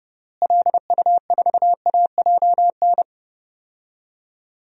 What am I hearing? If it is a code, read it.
Morse code, heard as a keyed tone, about 30 wpm, 700 Hz. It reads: LU4AJN
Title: DXCC-100 Call Signs - 6 Characters - Repeated 2x - 30wpm